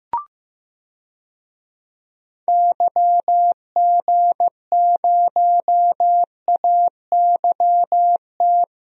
Morse code, heard as a keyed tone, about 15 wpm, 700 Hz. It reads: YG0AYT
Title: DXCC-100 Call Signs - 6 Characters - Repeated 3x - 15wpm